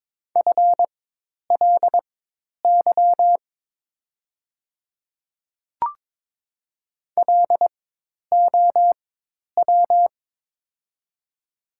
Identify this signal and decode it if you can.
Morse code, heard as a keyed tone, about 22 wpm, spaced as for 10 wpm, 690 Hz. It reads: FLY LOW
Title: 4x Character Spacing - Top 500 Words - 22wpm